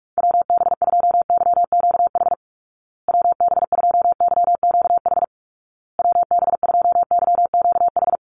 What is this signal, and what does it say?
Morse code, heard as a keyed tone, about 44 wpm, 700 Hz. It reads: WB2YQH WB2YQH WB2YQH
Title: Call Signs Repeated 2x - 45wpm